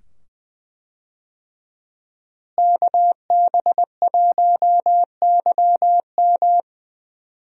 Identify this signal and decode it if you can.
Morse code, heard as a keyed tone, about 20 wpm, 700 Hz. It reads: KB1YM
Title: USA Call Signs - 5 Characters - Repeated 2x - 20wpm